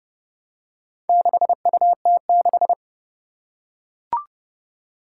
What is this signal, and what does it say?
Morse code, heard as a keyed tone, about 29 wpm, 690 Hz. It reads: 6UT6
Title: Four Challenging Letter-Number Combinations 30wpm